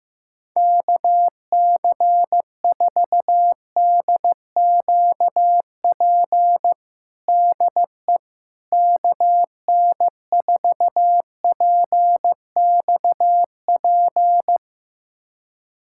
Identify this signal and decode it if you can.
Morse code, heard as a keyed tone, about 15 wpm, 700 Hz. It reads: KC4DQP DE KN4PXP
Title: USA Callsign DE Callsign 15wpm